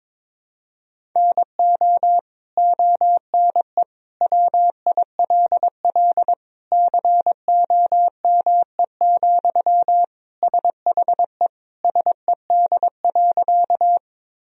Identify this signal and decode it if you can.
Morse code, heard as a keyed tone, about 21 wpm, 700 Hz. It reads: NO ONE WILL COME, SHE SED.
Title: Sentences from Top 100 Words and Abbreviations 22wpm